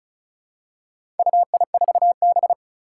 Morse code, heard as a keyed tone, about 36 wpm, 700 Hz. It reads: UI4B
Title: Four Challenging Letter-Number Combinations 35wpm